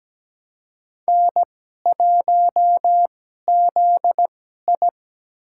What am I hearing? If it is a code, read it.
Morse code, heard as a keyed tone, about 17 wpm, spaced as for 12 wpm, 700 Hz. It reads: N1ZI
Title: USA Call Signs - 2x Character Spacing - 17wpm